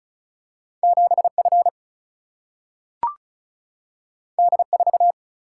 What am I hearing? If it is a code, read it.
Morse code, heard as a keyed tone, about 36 wpm, 700 Hz. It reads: 7 F D 4
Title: Single-Letter Single-Number 35wpm